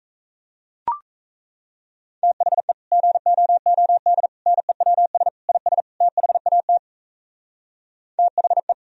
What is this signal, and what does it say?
Morse code, heard as a keyed tone, about 38 wpm, 700 Hz. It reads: THE GOOD NEWS IS THAT THE